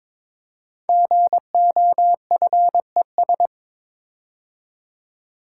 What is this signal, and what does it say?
Morse code, heard as a keyed tone, about 22 wpm, 700 Hz. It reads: GOFES
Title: Code Groups - 5 Letters - 22wpm